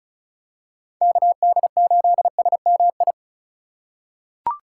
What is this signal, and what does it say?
Morse code, heard as a keyed tone, about 34 wpm, 700 Hz. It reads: KD8SMI
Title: USA Call Signs - 3x Character Spacing - 35wpm